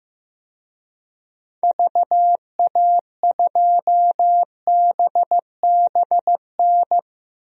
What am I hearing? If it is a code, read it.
Morse code, heard as a keyed tone, about 15 wpm, 700 Hz. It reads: VA2BBN
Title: DXCC-100 Call Signs - 6 Characters - 15wpm